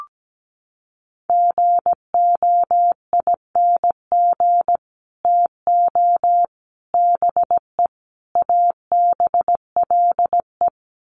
Morse code, heard as a keyed tone, about 17 wpm, 700 Hz. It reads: GOING TO BE ABLE
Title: Sets of 4 Words 17wpm